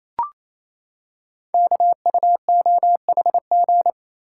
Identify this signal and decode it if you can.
Morse code, heard as a keyed tone, about 27 wpm, 700 Hz. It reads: KUOHG
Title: Code Groups - 5 Letters - 28wpm